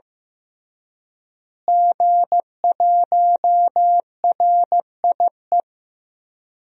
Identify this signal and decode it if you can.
Morse code, heard as a keyed tone, about 15 wpm, 700 Hz. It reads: G1RIE
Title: DXCC-100 Call Signs - 5 Characters - Repeated 3x - 15wpm